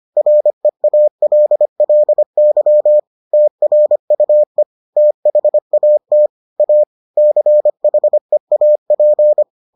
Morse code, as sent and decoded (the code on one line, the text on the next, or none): .-. .- .-.. .-.. -.-- / - .-. ..- . / - .... .- - / .- / -.-. .... .- .--.
REALLY TRUE THAT A CHEAP